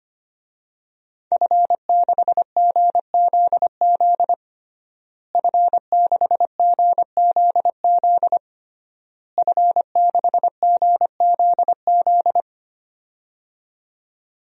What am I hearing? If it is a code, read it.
Morse code, heard as a keyed tone, about 25 wpm, 700 Hz. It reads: F6GZZ F6GZZ F6GZZ
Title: DXCC-100 Call Signs - 5 Characters - Repeated 3x - 25wpm